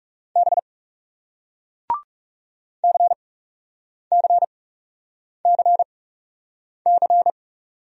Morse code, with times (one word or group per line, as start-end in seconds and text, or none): 0.4-0.6 B
2.8-7.3 CCCC